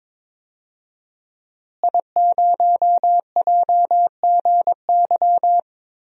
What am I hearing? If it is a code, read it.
Morse code, heard as a keyed tone, about 22 wpm, 700 Hz. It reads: I0JGY
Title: DXCC-100 Call Signs - 5 Characters - Repeated 2x - 22wpm